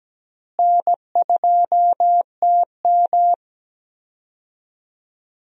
Morse code, heard as a keyed tone, about 17 wpm, 700 Hz. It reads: N2TM
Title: USA Call Signs - 4 Characters - Repeated 3x - 17wpm